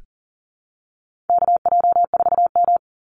Morse code, heard as a keyed tone, about 40 wpm, 700 Hz. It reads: KJ4M
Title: Call Signs Repeated 2x - 40wpm